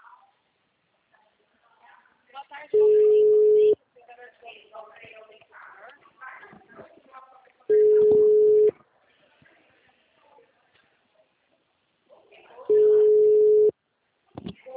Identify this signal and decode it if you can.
Ringback tone (Europe, 425 Hz), with speech.